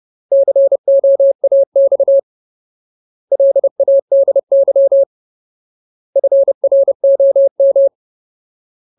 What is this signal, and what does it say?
Morse code, heard as a keyed tone, about 30 wpm, 550 Hz. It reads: COAX LADY FROM
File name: Ultimate Word List / 30WPM Ultimate